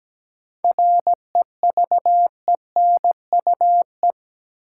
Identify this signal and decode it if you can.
Morse code, heard as a keyed tone, about 17 wpm, 700 Hz. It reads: REVENUE
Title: Seven Letter Words - 17wpm